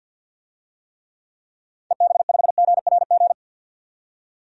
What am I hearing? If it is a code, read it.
Morse code, heard as a keyed tone, about 52 wpm, 700 Hz. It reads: EB5GRG
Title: DXCC-100 Call Signs - 6 Characters - 50wpm